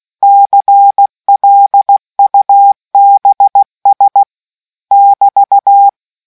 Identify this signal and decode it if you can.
Morse code, heard as a keyed tone, about 16 wpm, 800 Hz. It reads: CLUBS <BT>